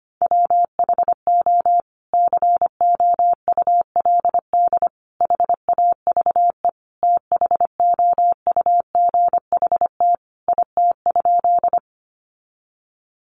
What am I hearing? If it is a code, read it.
Morse code, heard as a keyed tone, about 25 wpm, 700 Hz. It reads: WHO COULD HAVE THOUGHT IT?
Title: Sentences from Top 300 Words 25wpm